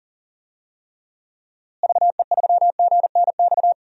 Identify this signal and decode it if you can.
Morse code, heard as a keyed tone, about 40 wpm, 700 Hz. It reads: VE3GNX
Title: DXCC-100 Call Signs - 6 Characters - Repeated 2x - 40wpm